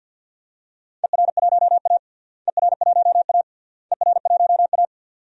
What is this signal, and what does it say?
Morse code, heard as a keyed tone, about 48 wpm, 700 Hz. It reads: ER1A ER1A ER1A
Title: DXCC-100 Call Signs - 4 Characters - Repeated 3x - 50wpm